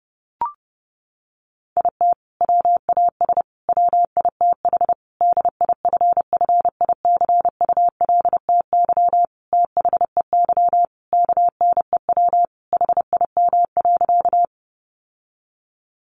Morse code, heard as a keyed tone, about 30 wpm, 700 Hz. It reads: IT WAS WITH DIFFICULTY THEY KNEW HIM.
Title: Sentences from Top 1500 Words 30wpm